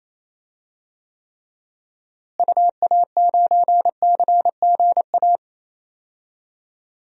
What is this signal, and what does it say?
Morse code, heard as a keyed tone, about 28 wpm, 700 Hz. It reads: UA9CGA
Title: DXCC-100 Call Signs - 6 Characters - Repeated 3x - 28wpm